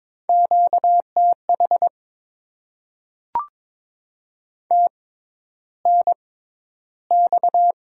Morse code, heard as a keyed tone, about 22 wpm, spaced as for 11 wpm, 700 Hz. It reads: QTH TNX
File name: Top 20 QSO Elements - 6x Character Spacing - 22wpm